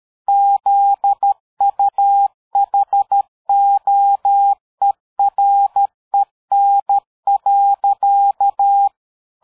Morse code, heard as a keyed tone, about 13 wpm, 800 Hz. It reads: ZUHOEREN.
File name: dk5ke / PersWorte 60